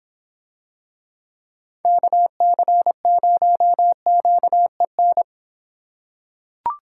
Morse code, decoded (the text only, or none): KC0QEN